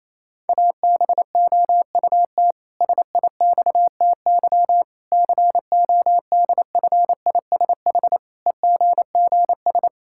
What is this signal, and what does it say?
Morse code, heard as a keyed tone, about 28 wpm, 700 Hz. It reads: ABOUT SIXTY CODFISH EGGS
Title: Pangram Challenge v2 - 28wpm